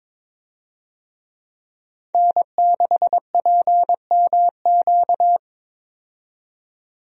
Morse code, heard as a keyed tone, about 22 wpm, 700 Hz. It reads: N6PMQ
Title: USA Call Signs - 5 Characters - 22wpm